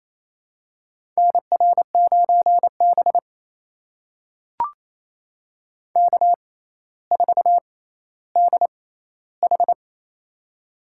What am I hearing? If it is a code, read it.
Morse code, heard as a keyed tone, about 28 wpm, spaced as for 14 wpm, 700 Hz. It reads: NR9B K4DH
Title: USA Call Signs - 4 Characters - 6x Character Spacing - 28wpm